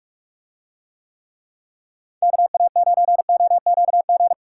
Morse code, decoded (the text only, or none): KA9OQG